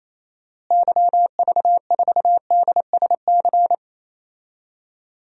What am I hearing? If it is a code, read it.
Morse code, heard as a keyed tone, about 27 wpm, 700 Hz. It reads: YV4DSC
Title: DXCC-100 Call Signs - 6 Characters - Repeated 2x - 28wpm